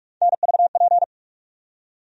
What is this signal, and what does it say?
Morse code, heard as a keyed tone, about 44 wpm, 700 Hz. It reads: NVP
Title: USA Call Sign Postfixes - 3 Letters 45wpm